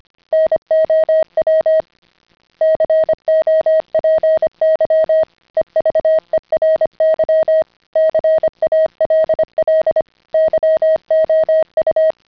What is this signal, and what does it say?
Morse code, heard as a keyed tone, about 25 wpm, 620 Hz. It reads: NOW COPY EVERY CALL YOU
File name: CW-pile up